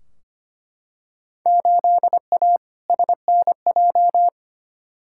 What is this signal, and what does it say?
Morse code, heard as a keyed tone, about 25 wpm, 700 Hz. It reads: 8A SNJ
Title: Field Day Exchanges Encore - 25wpm